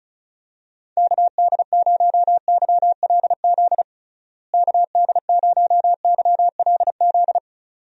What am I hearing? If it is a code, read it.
Morse code, heard as a keyed tone, about 36 wpm, 700 Hz. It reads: KD0YLZ KD0YLZ
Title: USA Call Signs - 6 Characters - Repeated 2x - 35wpm